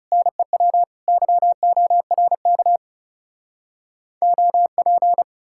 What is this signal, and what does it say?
Morse code, heard as a keyed tone, about 40 wpm, 700 Hz. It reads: NEW YORK OP